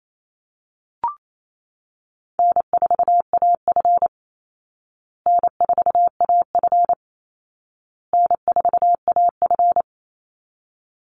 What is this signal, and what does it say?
Morse code, heard as a keyed tone, about 28 wpm, 700 Hz. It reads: N4AF N4AF N4AF
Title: Easy Call Signs Repeated 3x - 28wpm